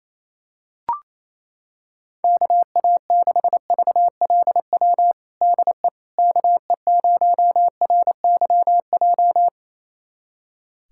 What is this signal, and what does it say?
Morse code, heard as a keyed tone, about 28 wpm, 700 Hz. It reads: KA6VLW DE KE0RYJ
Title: USA Callsign DE Callsign - 6 Characters - 28wpm